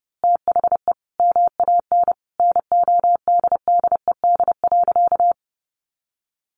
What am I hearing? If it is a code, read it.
Morse code, heard as a keyed tone, about 30 wpm, 700 Hz. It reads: THE MAN NODDED.
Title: A Sound Of Thunder 30wpm